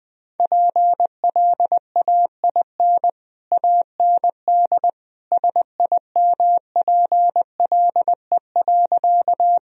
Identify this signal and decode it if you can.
Morse code, heard as a keyed tone, about 21 wpm, 700 Hz. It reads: PLAIN AND SIMPLE.
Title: Sentences from Top 1000 Words - Review Part 2 - 20wpm